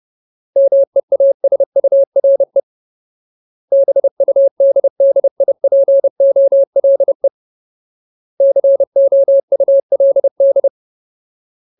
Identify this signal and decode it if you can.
Morse code, heard as a keyed tone, about 30 wpm, 550 Hz. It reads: MEASURE BUDDIPOLE COULD
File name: Ultimate Word List / 30WPM Ultimate